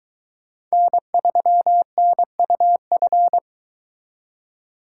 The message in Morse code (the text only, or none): N3NUF